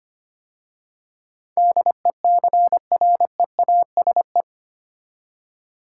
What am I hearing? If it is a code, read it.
Morse code, heard as a keyed tone, about 25 wpm, 700 Hz. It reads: DECREASE
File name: 8 Letter Words - First 3 Letters Given - 25wpm